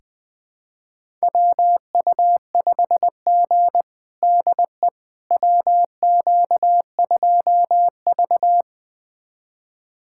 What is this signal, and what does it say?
Morse code, heard as a keyed tone, about 20 wpm, 700 Hz. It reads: WU5G DE WQ2V
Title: USA Callsign DE Callsign - 4 Characters - 20wpm